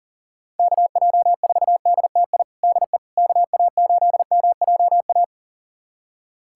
Morse code, heard as a keyed tone, about 40 wpm, 700 Hz. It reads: KJ4DTI DE KA8MJA
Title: USA Callsign DE Callsign 40wpm